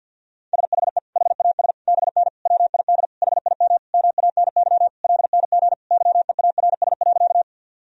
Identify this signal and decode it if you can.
Morse code, heard as a keyed tone, about 50 wpm, 700 Hz. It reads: SHE HAS BN WID HIM MANY LNG YEARS.
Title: Sentences from Top 200 Words and Abbreviations 50wpm